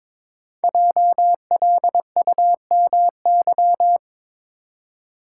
Morse code, heard as a keyed tone, about 22 wpm, 690 Hz. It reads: JLUMY
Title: Code Groups - 5 Letters - 22wpm